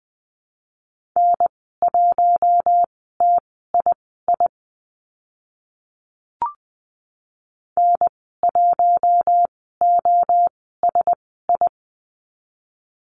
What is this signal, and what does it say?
Morse code, heard as a keyed tone, about 20 wpm, spaced as for 15 wpm, 700 Hz. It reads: N1TII N1OSI
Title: 042 Lesson 8g - Number 1 - Call Signs 20wpm